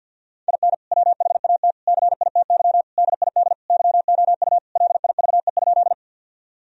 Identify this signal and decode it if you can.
Morse code, heard as a keyed tone, about 51 wpm, 700 Hz. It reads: IN WHAT CITY DID YOU LIVE?
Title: Sentences from Top 1000 Words - Review Part 1 - 50wpm